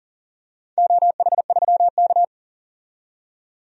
Morse code, heard as a keyed tone, about 40 wpm, 700 Hz. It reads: OH3K
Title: DXCC-100 Call Signs - 4 Characters - 40wpm